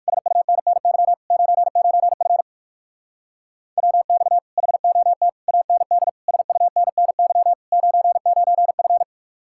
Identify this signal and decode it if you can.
Morse code, heard as a keyed tone, about 60 wpm, 700 Hz. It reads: SUNNY 99F WX HOT AND SUNNY 99F